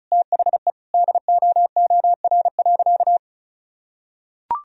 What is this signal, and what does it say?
Morse code, heard as a keyed tone, about 36 wpm, 700 Hz. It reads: THE DOOR.
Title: Sentences from Top 1000 Words - Review Part 1 - 35wpm